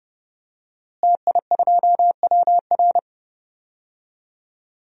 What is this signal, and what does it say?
Morse code, heard as a keyed tone, about 30 wpm, 700 Hz. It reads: TI2WR
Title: DXCC-100 Call Signs - 5 Characters - Repeated 3x - 30wpm